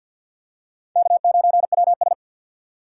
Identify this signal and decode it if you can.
Morse code, heard as a keyed tone, about 52 wpm, 700 Hz. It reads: K9WS
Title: USA Call Signs - 4 Characters - 2x Character Spacing - 50wpm